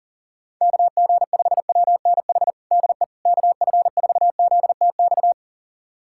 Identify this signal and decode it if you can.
Morse code, heard as a keyed tone, about 40 wpm, 700 Hz. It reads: KG5WNH DE KF4ZTX
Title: USA Callsign DE Callsign 40wpm